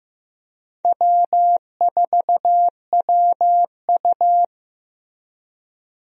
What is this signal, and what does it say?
Morse code, heard as a keyed tone, about 15 wpm, 700 Hz. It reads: W4WU
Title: USA Call Signs - 4 Characters - Repeated 2x - 15wpm